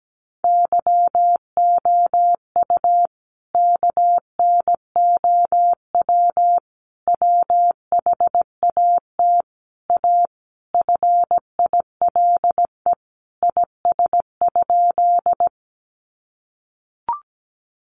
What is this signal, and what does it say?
Morse code, heard as a keyed tone, about 17 wpm, 700 Hz. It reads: YOU KNOW WHAT A FILE IS?
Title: Sentences from Top 1200 Words 17wpm